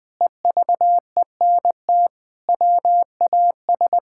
Morse code, heard as a keyed tone, about 20 wpm, 700 Hz. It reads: EVENT WAS
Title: Sentences with 7-Word Phrases - Spoken Beginning - 20wpm